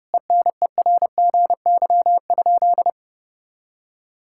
Morse code, heard as a keyed tone, about 30 wpm, 700 Hz. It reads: ENERGY?
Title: Sentences from Top 1000 Words - Review Part 6 - 30wpm